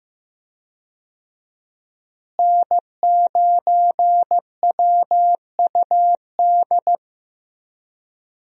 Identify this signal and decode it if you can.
Morse code, heard as a keyed tone, about 15 wpm, 700 Hz. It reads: N9WUD
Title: USA Call Signs - 5 Characters - Repeated 3x - 15wpm